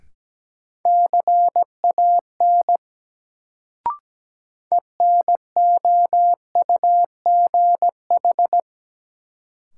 Morse code, heard as a keyed tone, about 17 wpm, 700 Hz. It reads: CAN ENOUGH
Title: Sets of 2 Words - New Word Drill - 17wpm